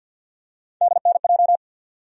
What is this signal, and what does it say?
Morse code, heard as a keyed tone, about 53 wpm, 700 Hz. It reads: DNJ
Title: USA Call Sign Postfixes - 3 Letters 50wpm